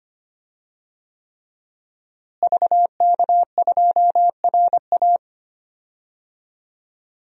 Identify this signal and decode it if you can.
Morse code, heard as a keyed tone, about 25 wpm, 700 Hz. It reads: VK2RA
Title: DXCC-100 Call Signs - 5 Characters - Repeated 3x - 25wpm